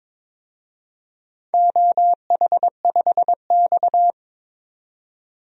Morse code, heard as a keyed tone, about 22 wpm, 700 Hz. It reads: OH5X
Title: DXCC-100 Call Signs - 4 Characters - 22wpm